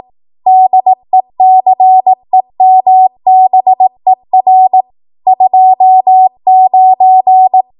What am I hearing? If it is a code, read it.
Morse code, heard as a keyed tone, about 18 wpm, 750 Hz. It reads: DECEMBER 29
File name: Archive / 18 WPM / 18WPM